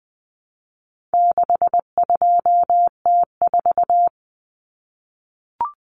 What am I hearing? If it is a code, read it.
Morse code, heard as a keyed tone, about 20 wpm, 700 Hz. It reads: 62T4
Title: Common Cut Numbers - 4-Digit 20wpm